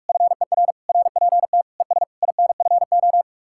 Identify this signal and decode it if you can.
Morse code, heard as a keyed tone, about 56 wpm, 700 Hz. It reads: FER RPT ES INFO
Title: Speed-Racing Common QSO Phrases - 45wpm